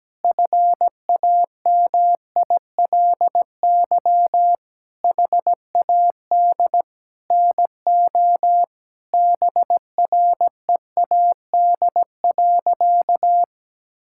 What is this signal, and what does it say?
Morse code, heard as a keyed tone, about 17 wpm, 700 Hz. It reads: FAMILY HAD NO BREAD.
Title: Sentences from Top 1600 Words 17wpm